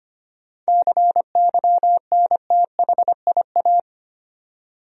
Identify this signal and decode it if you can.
Morse code, heard as a keyed tone, about 25 wpm, 700 Hz. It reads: CYNTHIA
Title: USA Common Names - 25wpm